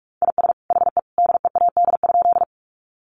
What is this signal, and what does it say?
Morse code, heard as a keyed tone, about 45 wpm, 700 Hz. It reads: IS HE DEAD?
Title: Sentences from Top 1500 Words 45wpm